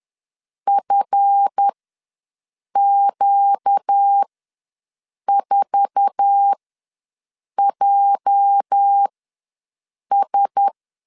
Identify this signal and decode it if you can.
Morse code, heard as a keyed tone, about 10 wpm, spaced as for 6 wpm, 800 Hz. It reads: FQ4JS
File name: LEZIONI / 8 YJ45 / 8 YJ4 ws 10wpm 4020